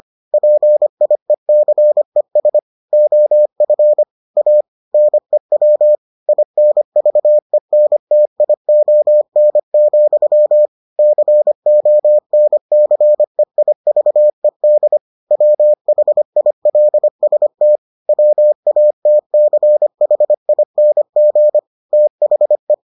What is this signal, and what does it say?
Morse code, heard as a keyed tone, about 25 wpm, 600 Hz. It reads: PIECES OF A NEW INVENTION, CONCEIVED WHILST WATCHING THE